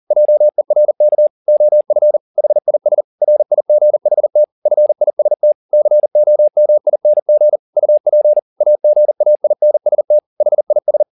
Morse code, heard as a keyed tone, about 40 wpm, 600 Hz. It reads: JERK OF HIS RIGHT FIST COMING UP AGAINST HIS
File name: the old man and the sea 2 40wpm